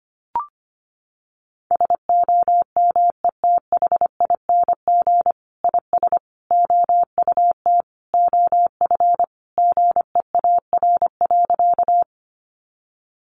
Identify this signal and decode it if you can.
Morse code, heard as a keyed tone, about 25 wpm, 700 Hz. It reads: SOMETHING IS OUT OF GEAR.
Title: Sentences from Top 1400 Words 25wpm